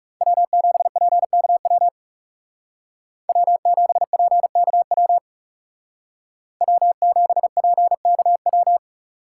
Morse code, heard as a keyed tone, about 39 wpm, 700 Hz. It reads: W7PKW W7PKW W7PKW